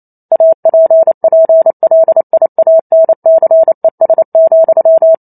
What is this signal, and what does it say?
Morse code, heard as a keyed tone, about 28 wpm, 650 Hz. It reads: APPLIANCES,
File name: Chapter 2 - 28 WPM